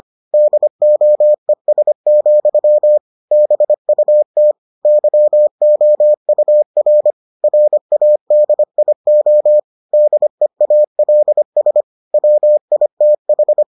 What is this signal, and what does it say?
Morse code, heard as a keyed tone, about 25 wpm, 600 Hz. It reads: DOES, BUT YOUR RADIO DEALS WITH